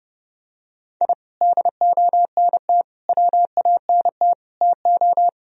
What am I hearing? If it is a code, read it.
Morse code, heard as a keyed tone, about 30 wpm, 700 Hz. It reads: I DONT WANT TO
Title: Sentences from Top 1000 Words - Review Part 4 - 30wpm